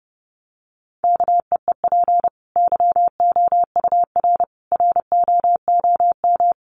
Morse code, heard as a keyed tone, about 30 wpm, 700 Hz. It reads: KEEP YOUR ROOM
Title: Sentences from Top 1100 Words 30wpm